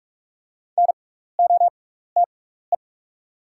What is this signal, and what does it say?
Morse code, heard as a keyed tone, about 44 wpm, spaced as for 15 wpm, 700 Hz. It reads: NOTE
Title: 6x Character Spacing - Top 500 Words - 45wpm